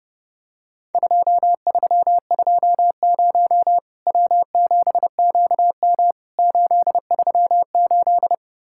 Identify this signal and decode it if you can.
Morse code, heard as a keyed tone, about 30 wpm, 700 Hz. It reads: 2320 W7QM 838
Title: Worked All Europe DX Contest - QTC Exchange - 30wpm